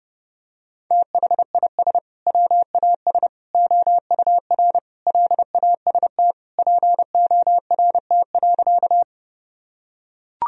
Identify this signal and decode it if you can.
Morse code, heard as a keyed tone, about 32 wpm, 700 Hz. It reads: THIS WAS OUR LAST PORT.
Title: Sentences from Top 1000 Words - Review Part 1 - 30wpm